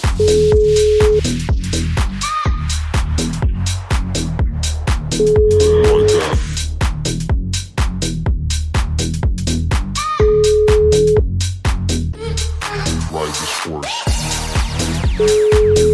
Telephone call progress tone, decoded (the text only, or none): ringback tone